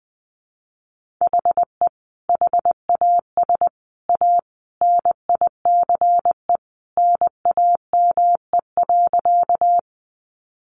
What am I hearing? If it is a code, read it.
Morse code, heard as a keyed tone, about 20 wpm, 700 Hz. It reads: HE HAS A NICE NAME.
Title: Sentences from Top 1100 Words 20wpm